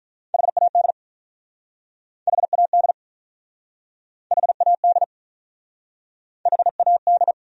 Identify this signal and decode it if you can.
Morse code, heard as a keyed tone, about 48 wpm, 700 Hz. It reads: HAD HAD HAD HAD